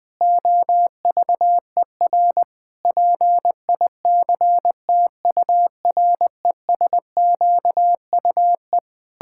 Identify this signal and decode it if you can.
Morse code, heard as a keyed tone, about 21 wpm, 700 Hz. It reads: OVER PICTURESQUE
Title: Pangram Challenge v2 - 20wpm